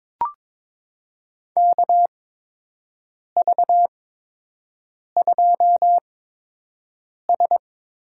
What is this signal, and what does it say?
Morse code, heard as a keyed tone, about 22 wpm, spaced as for 6 wpm, 700 Hz. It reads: KV2S
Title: USA Call Signs - 4 Characters - 8x Character Spacing - 22wpm